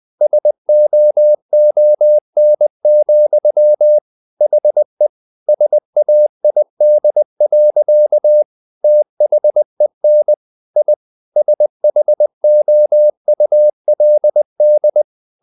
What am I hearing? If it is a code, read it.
Morse code, heard as a keyed tone, about 20 wpm, 600 Hz. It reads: SOON, HE SAID. THEN I SHOULD
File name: the old man and the sea part 2 20wpm